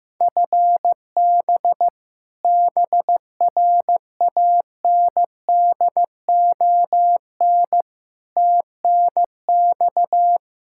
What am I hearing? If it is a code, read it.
Morse code, heard as a keyed tone, about 15 wpm, 700 Hz. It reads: FB BRANDON TNX
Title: Practice QSO - Exchange 3 - More Info - 15wpm